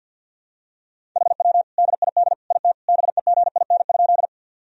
Morse code, heard as a keyed tone, about 50 wpm, 700 Hz. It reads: HW DID IT BEGIN?
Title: Sentences from Top 300 Words and Abbreviations 50wpm